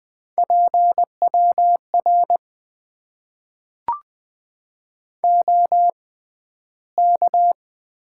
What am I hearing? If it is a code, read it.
Morse code, heard as a keyed tone, about 20 wpm, spaced as for 12 wpm, 700 Hz. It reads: PWR OK
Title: Top 90 QSO Elements - 6x Character Spacing - 20wpm